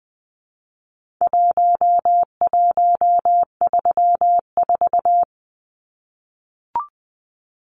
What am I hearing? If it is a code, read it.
Morse code, heard as a keyed tone, about 20 wpm, 700 Hz. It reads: 1134
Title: Numbers - Contest Style - 4-Digit 20wpm